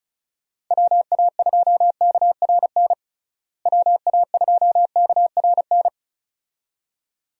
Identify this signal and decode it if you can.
Morse code, heard as a keyed tone, about 34 wpm, 700 Hz. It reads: WA2KRN WA2KRN
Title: USA Call Signs - 6 Characters - Repeated 2x - 35wpm